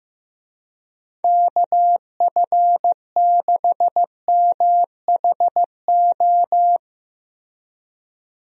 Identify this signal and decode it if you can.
Morse code, heard as a keyed tone, about 15 wpm, 700 Hz. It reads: KF6MHO
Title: USA Call Signs - 6 Characters - Repeated 2x - 15wpm